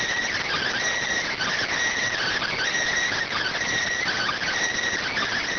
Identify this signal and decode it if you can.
Emergency Alert System (SAME) tones, heard as harsh buzzing bursts. It is heard from 0.3 to 4.5 s.